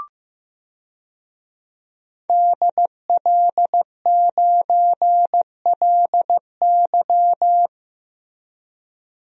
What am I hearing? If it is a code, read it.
Morse code, heard as a keyed tone, about 15 wpm, 700 Hz. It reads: DL9LY
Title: DXCC-100 Call Signs - 5 Characters - Repeated 2x - 15wpm